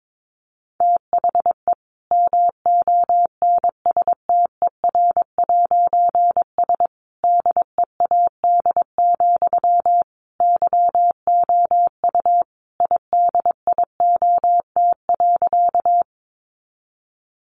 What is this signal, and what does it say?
Morse code, heard as a keyed tone, about 22 wpm, 700 Hz. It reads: THE MONSTER'S DEAD, YOU IDIOT.
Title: A Sound Of Thunder 22wpm